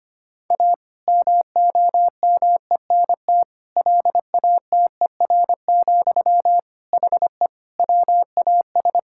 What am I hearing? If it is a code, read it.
Morse code, heard as a keyed tone, about 25 wpm, 700 Hz. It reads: A MOMENT LATER, HE WAS
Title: Sentences from Top 1000 Words - Review Part 6 - 25wpm